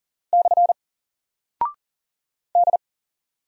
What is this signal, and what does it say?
Morse code, heard as a keyed tone, about 40 wpm, 700 Hz. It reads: / D
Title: Single Character (Letter:Number:Punctuation:Prosign) 40wpm